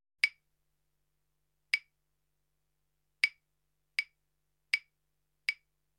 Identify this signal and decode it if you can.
There is no code.